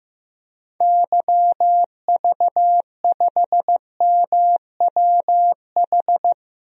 Morse code, heard as a keyed tone, about 15 wpm, 700 Hz. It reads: YV5MWH
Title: DXCC-100 Call Signs - 6 Characters - Repeated 2x - 15wpm